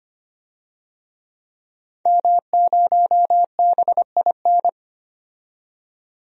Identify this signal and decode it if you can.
Morse code, heard as a keyed tone, about 25 wpm, 700 Hz. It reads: M0BIN
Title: DXCC-100 Call Signs - 5 Characters - 25wpm